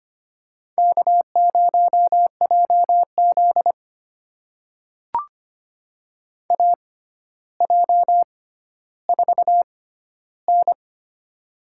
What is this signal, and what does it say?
Morse code, heard as a keyed tone, about 25 wpm, spaced as for 13 wpm, 700 Hz. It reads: K0JZ AJ4N
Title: USA Call Signs - 4 Characters - 6x Character Spacing - 25wpm